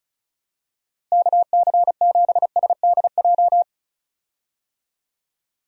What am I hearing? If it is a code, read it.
Morse code, heard as a keyed tone, about 34 wpm, 700 Hz. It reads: KC7SDJ
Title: USA Call Signs - 6 Characters - Repeated 3x - 35wpm